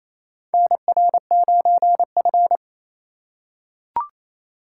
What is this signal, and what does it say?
Morse code, heard as a keyed tone, about 28 wpm, 700 Hz. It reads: NR9F
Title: USA Call Signs - 4 Characters - 3x Character Spacing - 28wpm